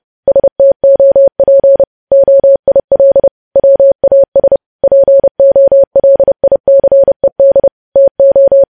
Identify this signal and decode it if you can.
Morse code, heard as a keyed tone, about 30 wpm, 550 Hz. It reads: STOP OIL WAS POLICED TO
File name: podcast / News headlines-1x-30x30